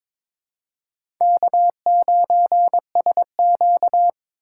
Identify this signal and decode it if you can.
Morse code, heard as a keyed tone, about 23 wpm, 700 Hz. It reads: K9SQ
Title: USA Call Signs - 4 Characters - 4x Character Spacing - 22wpm